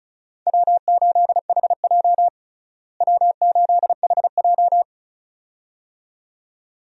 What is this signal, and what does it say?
Morse code, heard as a keyed tone, about 36 wpm, 700 Hz. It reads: W8HJ W8HJ
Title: USA Call Signs - 4 Characters - Repeated 2x - 35wpm